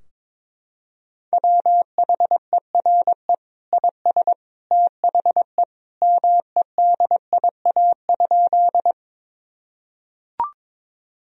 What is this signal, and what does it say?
Morse code, heard as a keyed tone, about 22 wpm, 700 Hz. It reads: WHERE IS THE MEDIA?
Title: Sentences from Top 1000 Words - Review Part 4 - 22wpm